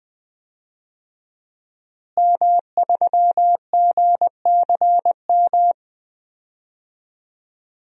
Morse code, heard as a keyed tone, about 20 wpm, 700 Hz. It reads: M3GCM
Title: DXCC-100 Call Signs - 5 Characters - Repeated 2x - 20wpm